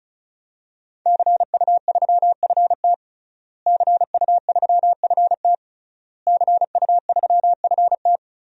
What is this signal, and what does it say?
Morse code, heard as a keyed tone, about 36 wpm, 700 Hz. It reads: CU3FT CU3FT CU3FT
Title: DXCC-100 Call Signs - 5 Characters - Repeated 3x - 35wpm